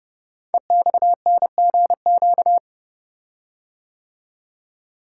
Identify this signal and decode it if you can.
Morse code, heard as a keyed tone, about 30 wpm, 700 Hz. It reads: EXNGQ
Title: Code Groups - 5 Letters - 30wpm